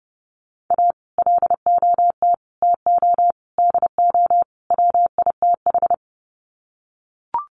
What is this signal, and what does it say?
Morse code, heard as a keyed tone, about 30 wpm, 700 Hz. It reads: A LOT TO DO WITH
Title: Sets of 5 Words 30wpm